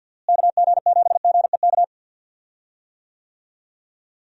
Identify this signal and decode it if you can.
Morse code, heard as a keyed tone, about 51 wpm, 700 Hz. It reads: KG7GEK
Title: USA Call Signs - 6 Characters - Repeated 2x - 50wpm